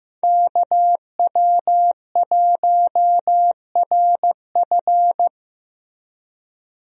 Morse code, heard as a keyed tone, about 15 wpm, 700 Hz. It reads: KW1RF
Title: USA Call Signs - 5 Characters - Repeated 2x - 15wpm